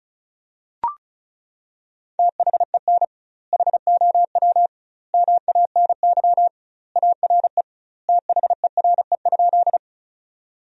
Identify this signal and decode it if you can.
Morse code, heard as a keyed tone, about 36 wpm, 700 Hz. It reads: THEN HOW MANY ARE THERE?
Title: 2x Word-Spacing - Sentences from Top 100 Words - 35wpm